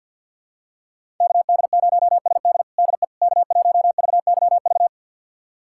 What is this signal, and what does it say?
Morse code, heard as a keyed tone, about 52 wpm, 700 Hz. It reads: KD0SD DE K1VYV
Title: USA Callsign DE Callsign 50wpm